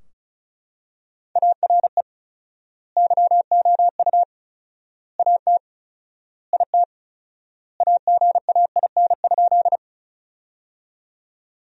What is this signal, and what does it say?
Morse code, heard as a keyed tone, about 36 wpm, 700 Hz. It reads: ARE YOU AT IT AGAIN?
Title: 4x Word-Spacing - Sentences from Top 200 Words - 35wpm